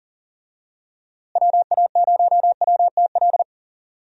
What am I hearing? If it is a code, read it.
Morse code, heard as a keyed tone, about 40 wpm, 700 Hz. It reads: WA0WTL
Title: USA Call Signs - 6 Characters - Repeated 2x - 40wpm